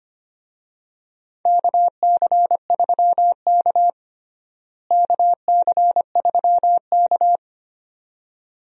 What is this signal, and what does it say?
Morse code, heard as a keyed tone, about 25 wpm, 700 Hz. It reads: KC3K KC3K
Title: USA Call Signs - 4 Characters - Repeated 2x - 25wpm